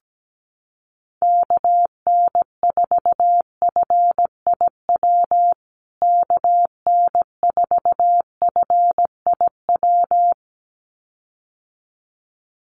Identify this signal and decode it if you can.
Morse code, heard as a keyed tone, about 17 wpm, 700 Hz. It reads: KN4FIW KN4FIW
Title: USA Call Signs Repeated 2x 17wpm